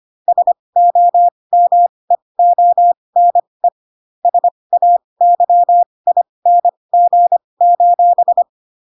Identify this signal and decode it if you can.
Morse code, heard as a keyed tone, about 25 wpm, spaced as for 20 wpm, 700 Hz. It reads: SOMEONE SAYING:
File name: Pinocchio / Pinocchio ch4 15 25 ik1hge